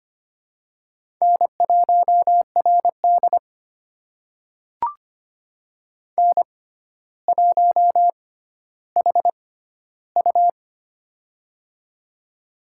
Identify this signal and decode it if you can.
Morse code, heard as a keyed tone, about 25 wpm, spaced as for 13 wpm, 700 Hz. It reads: N1RD N1HU
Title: USA Call Signs - 6x Character Spacing - 25wpm